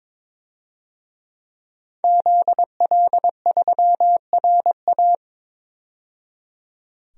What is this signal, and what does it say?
Morse code, heard as a keyed tone, about 22 wpm, 700 Hz. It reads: ZL3RA
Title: DXCC-100 Call Signs - 5 Characters - 22wpm